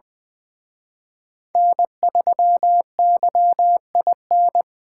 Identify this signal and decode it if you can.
Morse code, heard as a keyed tone, about 24 wpm, 700 Hz. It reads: N3YIN
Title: Speed Racing - USA 1x3 Call Signs 20wpm